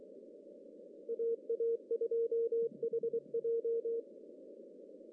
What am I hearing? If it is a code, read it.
Morse code, heard as a keyed tone, about 29 wpm, 470 Hz. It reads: AA2HJ
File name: CW / cqww-cw-k5tr